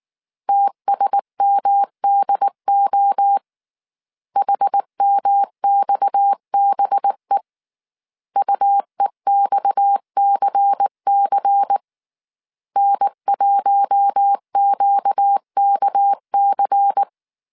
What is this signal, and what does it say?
Morse code, heard as a keyed tone, about 19 wpm, 800 Hz. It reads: TSMDO HMXBE UEXCC N1QKC
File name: CW / DET / 1 6 1500C S C 18wpm 2356